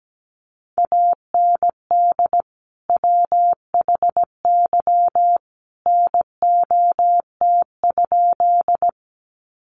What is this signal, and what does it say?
Morse code, heard as a keyed tone, about 17 wpm, 700 Hz. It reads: AND WHY NOT?
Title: Sentences from Top 600 Words 17wpm